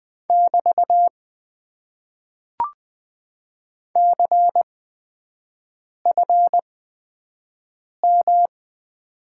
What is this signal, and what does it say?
Morse code, heard as a keyed tone, about 20 wpm, spaced as for 5 wpm, 700 Hz. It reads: <BT> CFM
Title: Top 90 QSO Elements - 8x Character Spacing - 20wpm